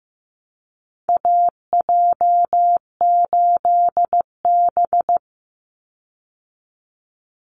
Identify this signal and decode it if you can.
Morse code, heard as a keyed tone, about 15 wpm, 700 Hz. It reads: AJ8B
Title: Call Signs Repeated 2x - 15wpm